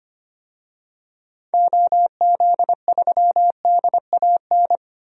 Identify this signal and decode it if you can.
Morse code, heard as a keyed tone, about 25 wpm, 690 Hz. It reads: OZ3DAN
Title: DXCC-100 Call Signs - 6 Characters - 25wpm